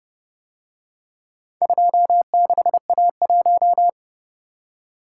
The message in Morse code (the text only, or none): 26A1